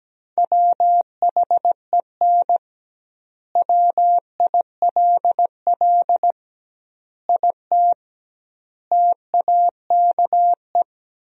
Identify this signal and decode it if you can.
Morse code, heard as a keyed tone, about 17 wpm, 700 Hz. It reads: WHEN WILL IT TAKE
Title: 2x Word-Spacing - Sentences from Top 200 Words - 17wpm